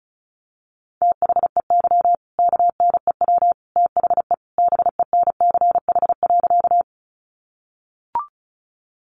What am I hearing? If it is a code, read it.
Morse code, heard as a keyed tone, about 34 wpm, 700 Hz. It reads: THEY KNEW THE BENCH.
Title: Sentences from Top 1400 Words 35wpm